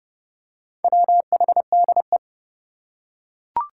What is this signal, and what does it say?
Morse code, heard as a keyed tone, about 30 wpm, 700 Hz. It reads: WHDE
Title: Four Challenging Letter-Number Combinations 30wpm